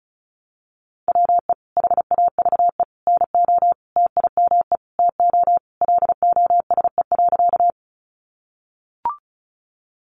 Morse code, heard as a keyed tone, about 34 wpm, 700 Hz. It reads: WE HAVE NO TIME TO LOSE.
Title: Sentences from Top 1100 Words 35wpm